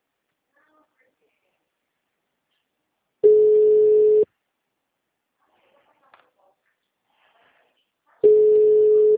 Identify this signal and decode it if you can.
Ringback tone (Europe, 425 Hz).